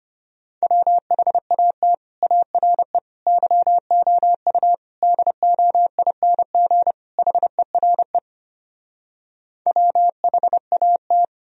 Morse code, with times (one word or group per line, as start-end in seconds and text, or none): 0.6-1.9 WHAT
2.2-3.0 ARE
3.3-4.7 YOU
5.0-6.9 DOING
7.2-8.2 HERE
9.7-11.2 WHAT